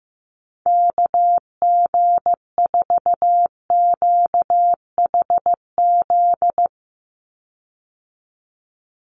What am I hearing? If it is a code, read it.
Morse code, heard as a keyed tone, about 15 wpm, 700 Hz. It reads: KG4QHZ
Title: USA Call Signs 15wpm